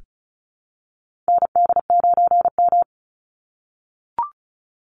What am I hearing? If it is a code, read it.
Morse code, heard as a keyed tone, about 34 wpm, 700 Hz. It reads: ND9M
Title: Call Signs - 35wpm